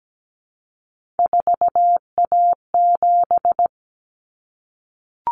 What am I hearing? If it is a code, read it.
Morse code, heard as a keyed tone, about 17 wpm, 700 Hz. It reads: 4A7
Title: Common Cut Numbers - 3-Digit 17wpm